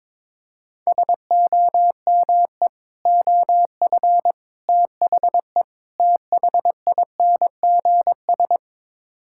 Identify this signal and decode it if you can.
Morse code, heard as a keyed tone, about 22 wpm, 700 Hz. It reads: SOME OF THE THINGS
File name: practice / 22wpm / Build A Phrase - 5-Word Phrases - 22wpm